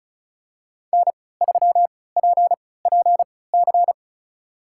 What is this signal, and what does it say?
Morse code, heard as a keyed tone, about 34 wpm, spaced as for 20 wpm, 700 Hz. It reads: N3PPC
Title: USA Call Signs - 5 Characters - 3x Character Spacing - 35wpm